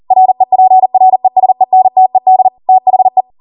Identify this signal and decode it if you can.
Morse code, heard as a keyed tone, about 40 wpm, 750 Hz. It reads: REPRESENTED THE